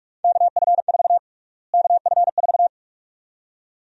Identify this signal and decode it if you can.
Morse code, heard as a keyed tone, about 46 wpm, 700 Hz. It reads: KF4 KF4
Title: USA Call Sign Prefixes - Repeated 2x - 45wpm